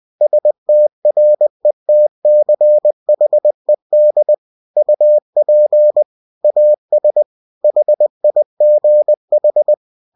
Morse code, heard as a keyed tone, about 20 wpm, 600 Hz. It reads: STRETCHED UP AS HIGH